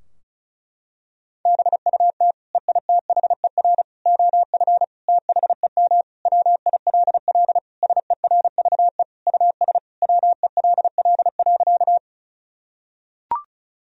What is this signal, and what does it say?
Morse code, heard as a keyed tone, about 36 wpm, 700 Hz. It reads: BUT EITHER OF THEM WILL SERVE US WELL.
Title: Sentences from Top 1000 Words - Review Part 4 - 35wpm